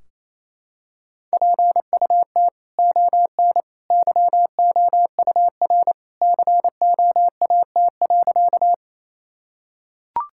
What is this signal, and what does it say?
Morse code, heard as a keyed tone, about 28 wpm, 700 Hz. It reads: PUT ON YOUR COAT.
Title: Sentences from Top 1600 Words 28wpm